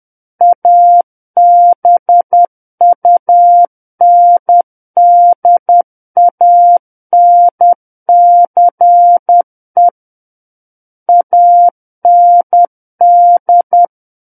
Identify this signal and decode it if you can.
Morse code, heard as a keyed tone, about 10 wpm, 700 Hz. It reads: ABUNDANCE AND
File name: Book One / 01-Chapter 1 - 10 WPM